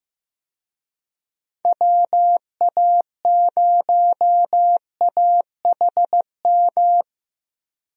Morse code, heard as a keyed tone, about 15 wpm, 700 Hz. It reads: WA0AHM
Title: USA Call Signs - 6 Characters - Repeated 3x - 15wpm